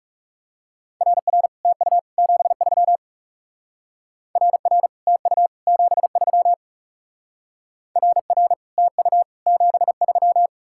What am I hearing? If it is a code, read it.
Morse code, heard as a keyed tone, about 40 wpm, 700 Hz. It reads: RR TU 73 RR TU 73 RR TU 73